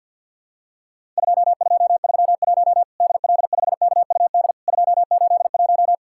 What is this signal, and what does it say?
Morse code, heard as a keyed tone, about 50 wpm, 700 Hz. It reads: 2231 DL5KUD 281
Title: Worked All Europe DX Contest - QTC Exchange with Repeats - 50wpm